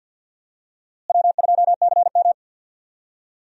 Morse code, heard as a keyed tone, about 51 wpm, 700 Hz. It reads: W2CM
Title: USA Call Signs - 4 Characters - Repeated 2x - 50wpm